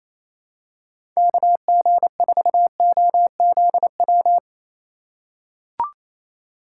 Morse code, heard as a keyed tone, about 28 wpm, 700 Hz. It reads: KG4OZW
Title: USA Call Signs - 6 Characters - 4x Character Spacing - 28wpm